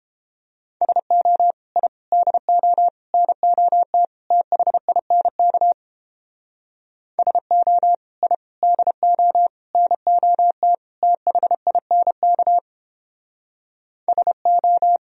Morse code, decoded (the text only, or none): SO I DO NOT THINK SO I DO NOT THINK SO